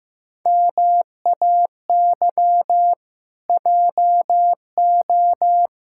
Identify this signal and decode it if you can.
Morse code, heard as a keyed tone, about 15 wpm, 700 Hz. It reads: MAY JO
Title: Pangram Challenge v2 - 15wpm